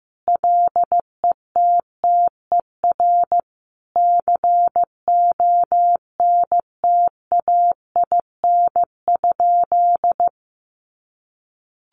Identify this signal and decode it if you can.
Morse code, heard as a keyed tone, about 15 wpm, 700 Hz. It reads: LETTER CONTAIN?
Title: Sentences from Top 500 Words 15wpm